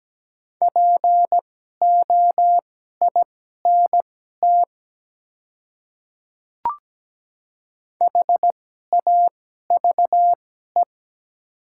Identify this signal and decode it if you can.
Morse code, heard as a keyed tone, about 17 wpm, spaced as for 12 wpm, 700 Hz. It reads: POINT HAVE
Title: 2x Character Spacing - Top 300 Words - 17wpm